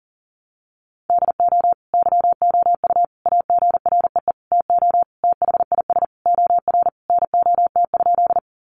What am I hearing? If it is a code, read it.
Morse code, heard as a keyed tone, about 37 wpm, 700 Hz. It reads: DO YOU AGREE TO THIS OR NOT?
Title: Sentences from Top 1100 Words 40wpm